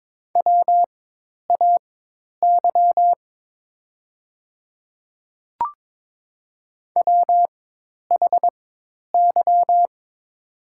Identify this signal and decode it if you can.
Morse code, heard as a keyed tone, about 22 wpm, spaced as for 10 wpm, 700 Hz. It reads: WAY WHY
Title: 4x Character Spacing - Top 500 Words - 22wpm